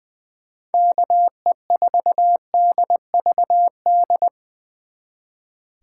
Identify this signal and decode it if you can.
Morse code, heard as a keyed tone, about 22 wpm, 700 Hz. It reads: KE4DVD
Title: Speed Racing - USA 2x3 Call Signs 20wpm